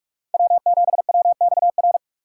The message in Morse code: W7WXR

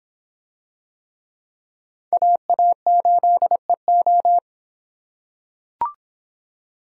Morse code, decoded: AA8EO

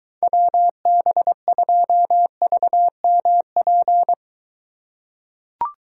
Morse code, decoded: WB2VMP

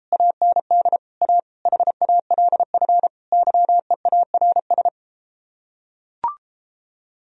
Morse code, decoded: AND A HALF YEARS

